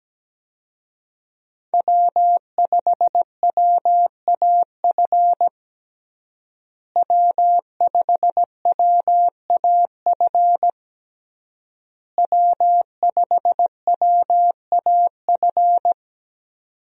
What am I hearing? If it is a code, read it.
Morse code, heard as a keyed tone, about 17 wpm, 700 Hz. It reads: W5WAF W5WAF W5WAF
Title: USA Call Signs - 5 Characters - Repeated 3x - 17wpm